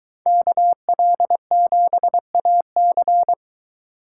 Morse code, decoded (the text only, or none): KL7AC